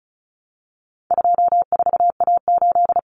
Morse code, heard as a keyed tone, about 34 wpm, 700 Hz. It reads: 24A8
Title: Common Cut Numbers - 4-Digit 35wpm